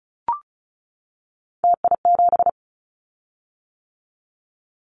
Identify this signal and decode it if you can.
Morse code, heard as a keyed tone, about 36 wpm, 700 Hz. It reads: TI7
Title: Callsign Prefixes for Top 100 DXCC - 35wpm